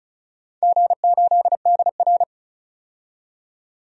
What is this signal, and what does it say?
Morse code, heard as a keyed tone, about 36 wpm, 690 Hz. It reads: G8DR
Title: DXCC-100 Call Signs - 4 Characters - 35wpm